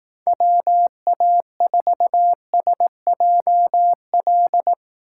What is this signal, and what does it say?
Morse code, heard as a keyed tone, about 18 wpm, 700 Hz. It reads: WA4SJL
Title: Speed Racing - USA 2x3 Call Signs 15wpm